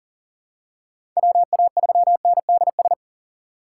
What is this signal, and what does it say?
Morse code, heard as a keyed tone, about 40 wpm, 700 Hz. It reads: WA3NDS
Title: USA Call Signs - 6 Characters - 2x Character Spacing - 40wpm